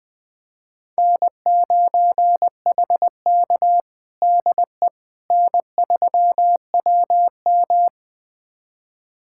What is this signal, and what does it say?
Morse code, heard as a keyed tone, about 20 wpm, 700 Hz. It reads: N9HK DE N3WM
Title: USA Callsign DE Callsign - 4 Characters - 20wpm